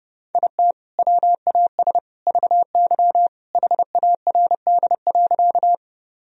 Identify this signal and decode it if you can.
Morse code, heard as a keyed tone, about 32 wpm, 700 Hz. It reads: IT WAS VY HARD.
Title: Sentences from Top 300 Words and Abbreviations 30wpm